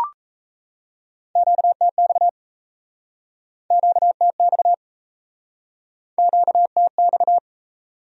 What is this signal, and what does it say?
Morse code, heard as a keyed tone, about 40 wpm, 700 Hz. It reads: QTX QTX QTX